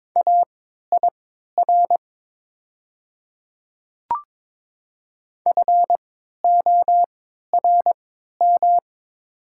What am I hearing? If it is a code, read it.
Morse code, heard as a keyed tone, about 22 wpm, spaced as for 12 wpm, 700 Hz. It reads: AIR FORM
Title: 3x Character Spacing - Top 200 Words - 22wpm